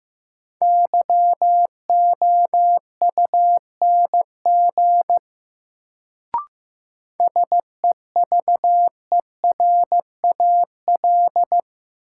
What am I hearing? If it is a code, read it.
Morse code, heard as a keyed tone, about 15 wpm, 700 Hz. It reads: YOUNG SEVERAL
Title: Sentences from Top 400 Words - New Word Drill - 15wpm